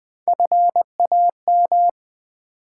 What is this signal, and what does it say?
Morse code, heard as a keyed tone, about 20 wpm, 700 Hz. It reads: FAM
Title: Build A Word - 3 Letters At A Time - 6-Letter Words - 20wpm